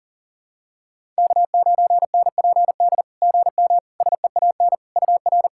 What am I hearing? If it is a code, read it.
Morse code, heard as a keyed tone, about 40 wpm, 690 Hz. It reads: K9NPD GM SEAN UR
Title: SOTA QSO Practice 40wpm